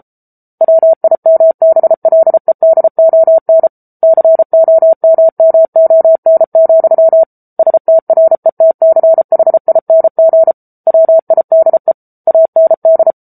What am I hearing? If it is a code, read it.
Morse code, heard as a keyed tone, about 34 wpm, 650 Hz. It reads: WIMBLEDON COMMON, STRETCHING WIDE AND
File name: Book Two / 07-Chapter 7 - 33 WPM